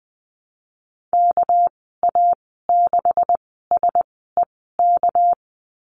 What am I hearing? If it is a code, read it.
Morse code, heard as a keyed tone, about 20 wpm, spaced as for 15 wpm, 700 Hz. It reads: KA6SEK
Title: 219 Lesson 35g - Number 0 - Call Signs 20wpm